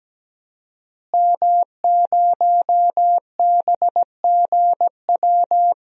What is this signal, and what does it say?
Morse code, heard as a keyed tone, about 17 wpm, 700 Hz. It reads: M0BGW
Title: DXCC-100 Call Signs - 5 Characters - Repeated 2x - 17wpm